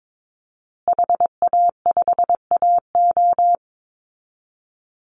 Morse code, heard as a keyed tone, about 22 wpm, 700 Hz. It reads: HA5AO
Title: Call Signs Repeated 3x - 22wpm